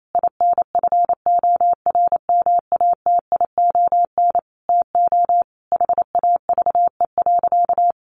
Morse code, heard as a keyed tone, about 28 wpm, 700 Hz. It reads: INFORMATION TO HAVE.
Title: Sentences from Top 1400 Words 28wpm